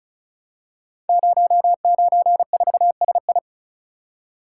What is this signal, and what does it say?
Morse code, heard as a keyed tone, about 36 wpm, 700 Hz. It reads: 094SI
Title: Code Groups - 5 Letter:Numbers - 35wpm